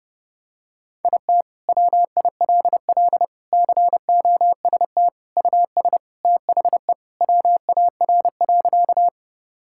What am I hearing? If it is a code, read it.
Morse code, heard as a keyed tone, about 29 wpm, 700 Hz. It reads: IT WILL COST US THE WAR.
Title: Sentences from Top 1000 Words - Review Part 5 - 30wpm